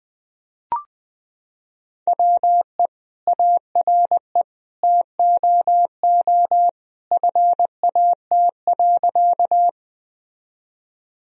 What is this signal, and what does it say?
Morse code, heard as a keyed tone, about 21 wpm, 700 Hz. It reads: WE ARE TOO FAT.
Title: Sentences from Top 1000 Words - Review Part 4 - 20wpm